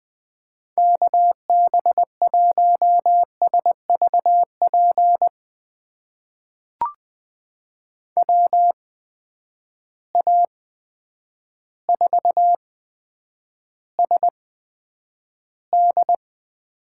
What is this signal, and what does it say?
Morse code, heard as a keyed tone, about 20 wpm, spaced as for 9 wpm, 700 Hz. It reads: KB1SVP WA4SD
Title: USA Call Signs - 6 Characters - 8x Character Spacing - 20wpm